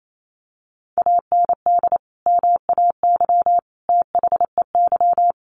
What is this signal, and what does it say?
Morse code, heard as a keyed tone, about 28 wpm, 700 Hz. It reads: AND MAY THEY
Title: Sentences from Top 1500 Words 28wpm